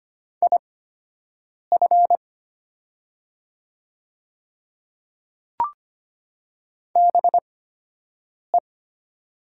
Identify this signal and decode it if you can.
Morse code, heard as a keyed tone, about 25 wpm, spaced as for 7 wpm, 700 Hz. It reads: IF BE
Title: 8x Character Spacing - Top 100 Words - 25wpm